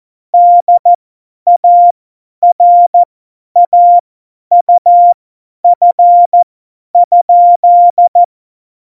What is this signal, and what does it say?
Morse code, heard as a keyed tone, about 14 wpm, spaced as for 10 wpm, 700 Hz. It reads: DARAUF?
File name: Morsekurs / Witze / Witz 21 April 2020, 50 F70